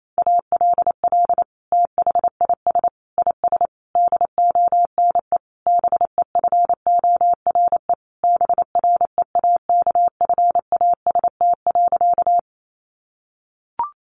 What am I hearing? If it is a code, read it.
Morse code, heard as a keyed tone, about 28 wpm, 700 Hz. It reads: ALL THIS IS DONE BEFORE BREAKFAST.
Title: Sentences from Top 1500 Words 28wpm